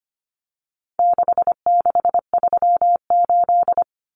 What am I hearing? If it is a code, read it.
Morse code, heard as a keyed tone, about 25 wpm, 700 Hz. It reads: 6638
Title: Numbers - 4-Digit 25wpm